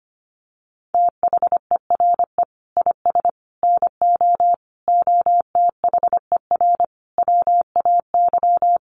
Morse code, heard as a keyed tone, about 25 wpm, 700 Hz. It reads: THERE IS NO OTHER WAY
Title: Sentences from Top 1300 Words 25wpm